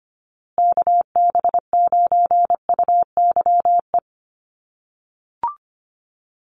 Morse code, heard as a keyed tone, about 25 wpm, 700 Hz. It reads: KB9UYE
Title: USA Call Signs Repeated 3x 25wpm